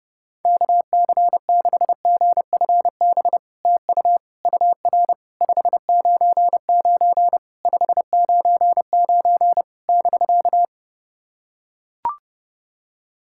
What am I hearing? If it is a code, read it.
Morse code, heard as a keyed tone, about 30 wpm, 700 Hz. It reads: KC6GFB TU UR 599 599 <BK>
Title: SOTA QSO Practice 30wpm